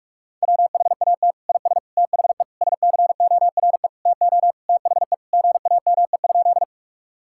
Morse code, heard as a keyed tone, about 43 wpm, 700 Hz. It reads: WHAT IS THE SCORE TO THE GAME?
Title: Sentences from Top 1600 Words 45wpm